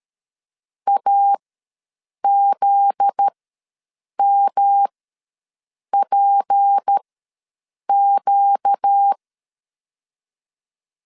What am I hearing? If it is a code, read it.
Morse code, heard as a keyed tone, about 13 wpm, spaced as for 7 wpm, 800 Hz. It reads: AZMPQ
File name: LEZIONI / 10 SIGNS / 10A SLASH ws 12wpm 3443